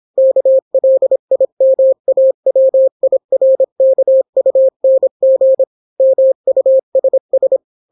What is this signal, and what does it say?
Morse code, heard as a keyed tone, about 26 wpm, 530 Hz. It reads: KLIMAWIRKUNG MUSS